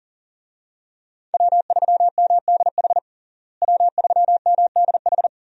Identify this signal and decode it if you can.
Morse code, heard as a keyed tone, about 40 wpm, 700 Hz. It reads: W3MDH W3MDH
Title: USA Call Signs - 5 Characters - Repeated 3x - 40wpm